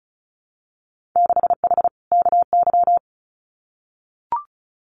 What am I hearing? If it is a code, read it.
Morse code, heard as a keyed tone, about 36 wpm, 700 Hz. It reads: 6H KY
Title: Winter Field Day Exchanges Encore 35wpm